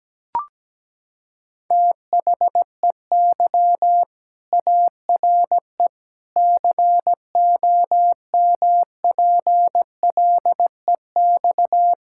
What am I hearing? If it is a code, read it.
Morse code, heard as a keyed tone, about 17 wpm, 700 Hz. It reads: THEY ARE COMPLEX
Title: Sentences from Top 1000 Words - Review Part 6 - 17wpm